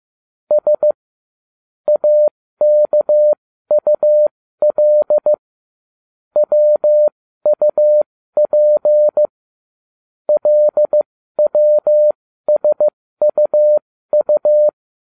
Morse code, heard as a keyed tone, about 15 wpm, spaced as for 13 wpm, 600 Hz. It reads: S AKUL WUP LWSUU